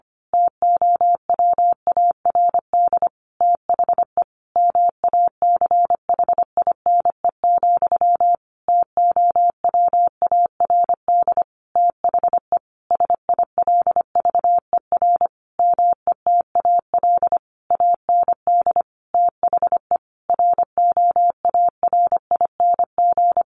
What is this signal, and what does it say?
Morse code, heard as a keyed tone, about 25 wpm, 700 Hz. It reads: TOWARD THE MACHINE, TOWARD THE SILVER METAL AND THE ROARING
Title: A Sound Of Thunder 25wpm